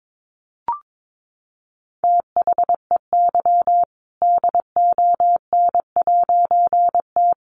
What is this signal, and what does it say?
Morse code, heard as a keyed tone, about 22 wpm, 700 Hz. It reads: THEY DON'T
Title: A Sound Of Thunder 22wpm